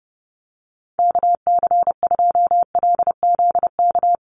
Morse code, heard as a keyed tone, about 30 wpm, 700 Hz. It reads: KC2LZK
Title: USA Call Signs 30wpm